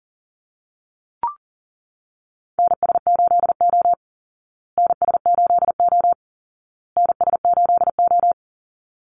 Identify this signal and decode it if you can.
Morse code, heard as a keyed tone, about 40 wpm, 700 Hz. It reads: NS8O NS8O NS8O
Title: Call Signs Repeated 2x - 40wpm